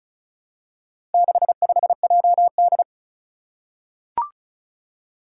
Morse code, heard as a keyed tone, about 34 wpm, 700 Hz. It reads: 65JD
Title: Four Challenging Letter-Number Combinations 35wpm